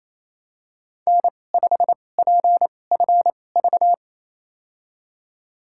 Morse code, heard as a keyed tone, about 28 wpm, spaced as for 20 wpm, 700 Hz. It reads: N5PFV
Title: USA Call Signs - 2x Character Spacing - 28wpm